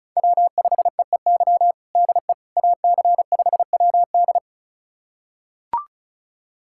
Morse code, heard as a keyed tone, about 36 wpm, 700 Hz. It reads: W5EEY DE AC5WD
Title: USA Callsign DE Callsign - 5 Characters - 35wpm